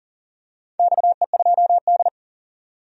Morse code, heard as a keyed tone, about 40 wpm, 700 Hz. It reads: XE2D
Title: DXCC-100 Call Signs - 4 Characters - 40wpm